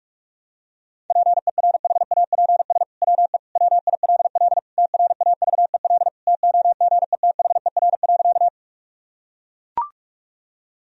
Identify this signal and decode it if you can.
Morse code, heard as a keyed tone, about 42 wpm, 700 Hz. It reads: PERHAPS WE WILL TRAVEL TOGETHER.
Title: Sentences from Top 1000 Words - Review Part 2 - 45wpm